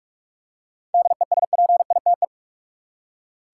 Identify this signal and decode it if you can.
Morse code, heard as a keyed tone, about 42 wpm, 700 Hz. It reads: DESPITE
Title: Seven Letter Words - 45wpm